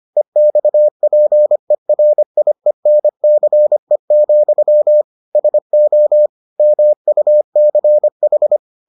Morse code, heard as a keyed tone, about 25 wpm, 600 Hz. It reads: EXPERIENCE, SO MUCH